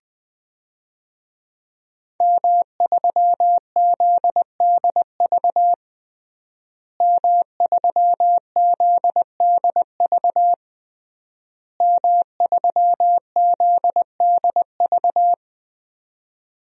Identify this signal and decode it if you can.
Morse code, heard as a keyed tone, about 20 wpm, 700 Hz. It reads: M3ZDV M3ZDV M3ZDV
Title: DXCC-100 Call Signs - 5 Characters - Repeated 3x - 20wpm